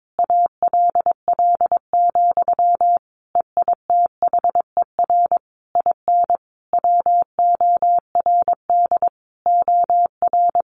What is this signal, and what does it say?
Morse code, heard as a keyed tone, about 22 wpm, 700 Hz. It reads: ALL, EITHER IN WORD OR
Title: Sentences from Top 1200 Words 22wpm